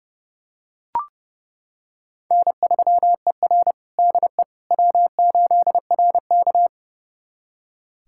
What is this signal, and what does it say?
Morse code, heard as a keyed tone, about 30 wpm, 700 Hz. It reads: N3ER DE W8RK
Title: USA Callsign DE Callsign - 4 Characters - 30wpm